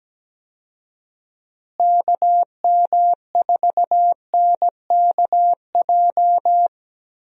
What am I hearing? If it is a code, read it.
Morse code, heard as a keyed tone, about 17 wpm, 700 Hz. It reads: KM4NKJ
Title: USA Call Signs - 6 Characters - Repeated 2x - 17wpm